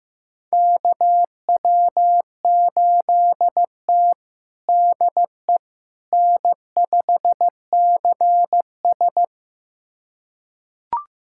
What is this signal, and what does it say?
Morse code, heard as a keyed tone, about 15 wpm, 700 Hz. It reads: KW8T DE N5CS
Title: USA Callsign DE Callsign - 4 Characters - 15wpm